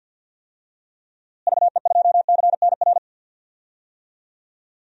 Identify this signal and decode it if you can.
Morse code, heard as a keyed tone, about 52 wpm, 700 Hz. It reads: VE2CNR
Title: DXCC-100 Call Signs - 6 Characters - Repeated 3x - 50wpm